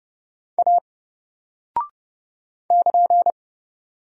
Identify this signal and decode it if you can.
Morse code, heard as a keyed tone, about 30 wpm, 700 Hz. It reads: A <KN>
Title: Single Character (Letter:Number:Punctuation:Prosign) 30wpm